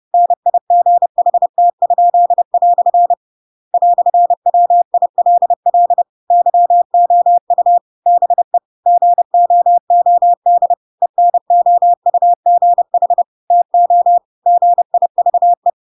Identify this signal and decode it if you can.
Morse code, heard as a keyed tone, about 30 wpm, 700 Hz. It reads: NIGHT?" "WILL YOU BE GOOD ENOUGH TO GIVE
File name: Pinocchio / Pinocchio ch6 30 30 ik1hge